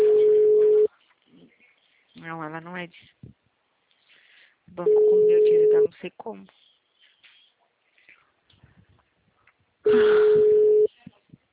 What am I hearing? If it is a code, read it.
Ringback tone (Europe, 425 Hz), with speech.
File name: out- -2012- - - .65796